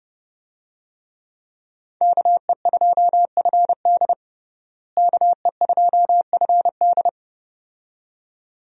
Morse code, heard as a keyed tone, about 30 wpm, 700 Hz. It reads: KE2FD KE2FD
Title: USA Call Signs - 5 Characters - Repeated 2x - 30wpm